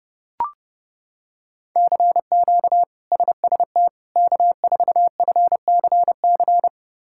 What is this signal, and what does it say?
Morse code, heard as a keyed tone, about 32 wpm, 700 Hz. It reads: CQ SST K4FCC
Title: K1USN SST QSO Practice - 30wpm